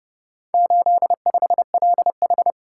Morse code, heard as a keyed tone, about 28 wpm, 700 Hz. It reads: 85LH